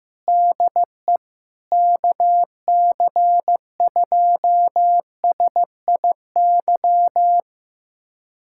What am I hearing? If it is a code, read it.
Morse code, heard as a keyed tone, about 15 wpm, 700 Hz. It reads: DE KC2SIY
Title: USA Callsign DE Callsign 15wpm